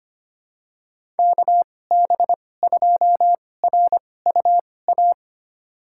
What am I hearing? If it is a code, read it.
Morse code, heard as a keyed tone, about 25 wpm, spaced as for 18 wpm, 700 Hz. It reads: KB2RUA
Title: USA Call Signs - 2x Character Spacing - 25wpm